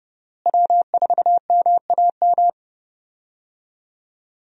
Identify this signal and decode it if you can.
Morse code, heard as a keyed tone, about 30 wpm, 700 Hz. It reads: W4MAM
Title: USA Call Signs - 5 Characters - Repeated 2x - 30wpm